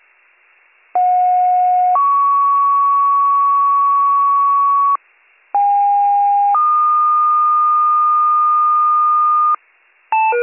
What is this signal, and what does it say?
Two-tone page, heard as a one-second tone, then a three-second tone, about 730 then 1120 Hz, about 810 then 1220 Hz. It is heard from 0.9 to 9.6 s.